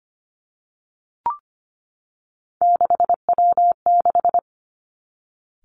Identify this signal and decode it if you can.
Morse code, heard as a keyed tone, about 25 wpm, 700 Hz. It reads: 6W6
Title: Three Challenging Letter-Number Combinations 25wpm